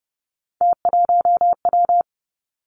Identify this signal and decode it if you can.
Morse code, heard as a keyed tone, about 30 wpm, 700 Hz. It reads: T1W
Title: Three Challenging Letter-Number Combinations 30wpm